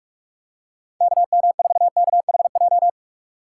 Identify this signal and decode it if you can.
Morse code, heard as a keyed tone, about 46 wpm, 690 Hz. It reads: KM4KHJ
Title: USA Call Signs - 4x Character Spacing - 45wpm